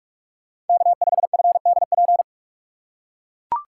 Morse code, heard as a keyed tone, about 44 wpm, 700 Hz. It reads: K5FDP
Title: USA Call Signs - 6x Character Spacing - 45wpm